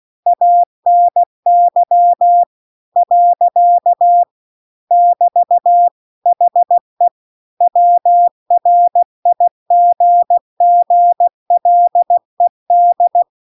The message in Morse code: ANY . <BT> HE WRIGGLED

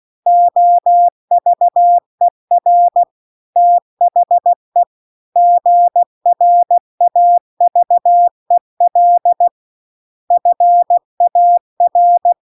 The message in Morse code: OVER THE GRAVEL FAR

